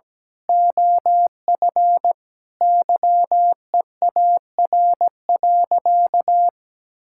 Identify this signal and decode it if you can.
Morse code, heard as a keyed tone, about 17 wpm, 700 Hz. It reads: OF YEAR.
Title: Sentences from Top 1000 Words - Review Part 6 - 17wpm